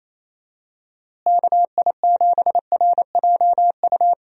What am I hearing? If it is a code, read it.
Morse code, heard as a keyed tone, about 27 wpm, 700 Hz. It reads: KI7RJU
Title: USA Call Signs - 6 Characters - Repeated 3x - 28wpm